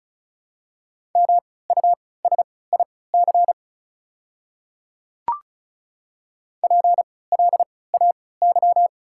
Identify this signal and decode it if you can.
Morse code, heard as a keyed tone, about 36 wpm, spaced as for 20 wpm, 690 Hz. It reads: MUSIC PLAY
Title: 3x Character Spacing - Top 300 Words - 35wpm